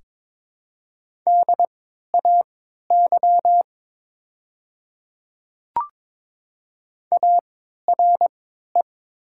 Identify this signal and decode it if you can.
Morse code, heard as a keyed tone, about 22 wpm, spaced as for 12 wpm, 700 Hz. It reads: DAY ARE
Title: 3x Character Spacing - Top 100 Words - 22wpm